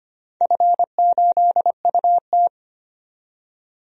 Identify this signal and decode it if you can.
Morse code, heard as a keyed tone, about 25 wpm, 700 Hz. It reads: F8UT
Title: DXCC-100 Call Signs - 4 Characters - 25wpm